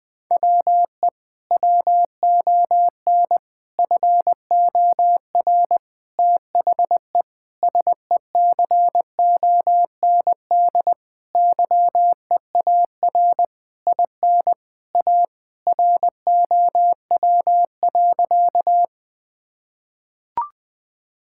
Morse code, heard as a keyed tone, about 21 wpm, 700 Hz. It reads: WE WON FOR THE SECOND YEAR IN A ROW.
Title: Sentences with 6-Word Phrases - Spoken Beginning - 20wpm